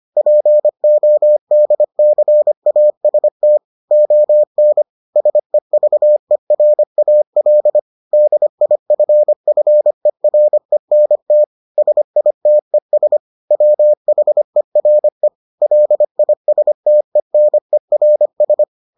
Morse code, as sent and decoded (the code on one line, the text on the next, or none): .--. --- -.. -.-. .- ... - / --- -. / ... ...- . .-. .- .-.. / -.. .. ..-. ..-. .-. -. - / ... .. - . ... / .-- .... .-. / .-.. .. ... - . -. .-. ...
PODCAST ON SEVERAL DIFFERENT SITES WHERE LISTENERS